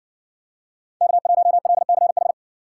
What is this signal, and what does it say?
Morse code, heard as a keyed tone, about 55 wpm, 700 Hz. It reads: K1LCH